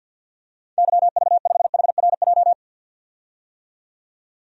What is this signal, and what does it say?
Morse code, heard as a keyed tone, about 50 wpm, 700 Hz. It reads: YV5HRJ
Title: DXCC-100 Call Signs - 6 Characters - 50wpm